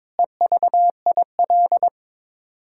Morse code, heard as a keyed tone, about 22 wpm, 700 Hz. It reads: EVIL
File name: Sets of 9 Words - 22wpm